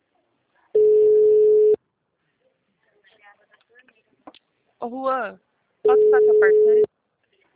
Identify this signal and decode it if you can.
Ringback tone (Europe, 425 Hz), with speech.